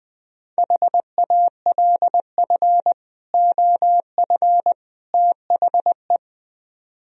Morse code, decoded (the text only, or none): HALF OF THE